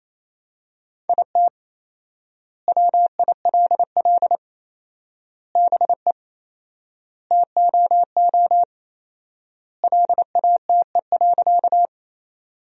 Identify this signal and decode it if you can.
Morse code, heard as a keyed tone, about 28 wpm, 700 Hz. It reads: IT WILL BE TOO LATE.
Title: 4x Word-Spacing - Sentences from Top 300 Words - 28wpm